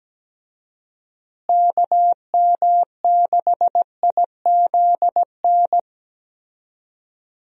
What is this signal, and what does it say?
Morse code, heard as a keyed tone, about 17 wpm, 700 Hz. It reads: KM6IZN
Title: USA Call Signs - 6 Characters - Repeated 3x - 17wpm